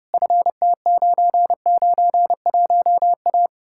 Morse code, heard as a keyed tone, about 40 wpm, spaced as for 35 wpm, 700 Hz. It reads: FT991A
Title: Speed-Racing Common QSO Phrases - 30wpm